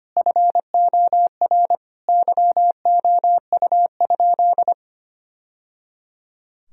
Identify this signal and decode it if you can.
Morse code, heard as a keyed tone, about 25 wpm, 700 Hz. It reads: FOR YOU?
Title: Sentences from Top 1000 Words - Review Part 2 - 25wpm